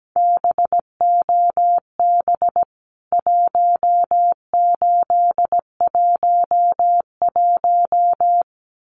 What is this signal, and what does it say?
Morse code, heard as a keyed tone, about 17 wpm, 700 Hz. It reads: BOB 1811
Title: CWT Exchange 17wpm